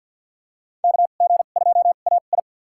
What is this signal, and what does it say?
Morse code, heard as a keyed tone, about 49 wpm, spaced as for 35 wpm, 700 Hz. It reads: KG2AI
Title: USA Call Signs - 5 Characters - 2x Character Spacing - 50wpm